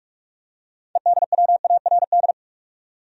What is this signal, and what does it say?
Morse code, heard as a keyed tone, about 46 wpm, 700 Hz. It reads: EDWARD
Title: USA Common Names - 45wpm